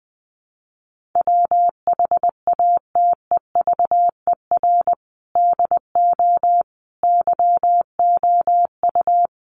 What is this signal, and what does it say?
Morse code, heard as a keyed tone, about 20 wpm, 700 Hz. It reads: WHATEVER DO YOU
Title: Sentences from Top 1100 Words 20wpm